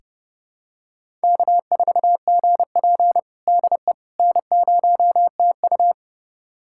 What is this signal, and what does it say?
Morse code, heard as a keyed tone, about 30 wpm, 700 Hz. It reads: K4GP DE N0TU
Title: USA Callsign DE Callsign - 4 Characters - 30wpm